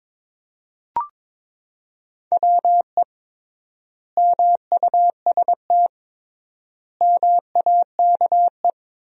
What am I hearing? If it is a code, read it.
Morse code, heard as a keyed tone, about 22 wpm, 700 Hz. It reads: WE MUST MAKE
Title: 3x Word-Spacing - Sentences from Top 400 Words - 22wpm